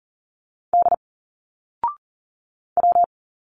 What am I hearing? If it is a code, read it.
Morse code, heard as a keyed tone, about 40 wpm, 700 Hz. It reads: D W
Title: Single Letters 40wpm